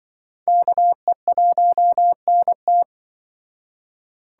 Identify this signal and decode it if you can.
Morse code, heard as a keyed tone, about 24 wpm, 700 Hz. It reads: KE1NT
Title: Speed Racing - USA 2x2 Call Signs 20wpm